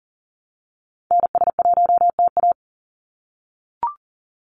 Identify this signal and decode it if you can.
Morse code, heard as a keyed tone, about 40 wpm, 700 Hz. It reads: NS1TA
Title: NA SOTA Activator Callsigns 40wpm